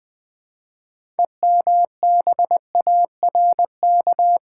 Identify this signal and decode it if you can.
Morse code, heard as a keyed tone, about 20 wpm, 700 Hz. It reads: EMBARK
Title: Build A Word - 2 Letters At A Time - 6-Letter Words - 20wpm